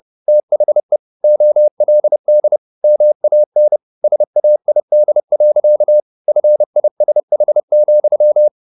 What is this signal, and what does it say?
Morse code, heard as a keyed tone, about 30 wpm, 600 Hz. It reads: THE OLD MAN SAID. FISH,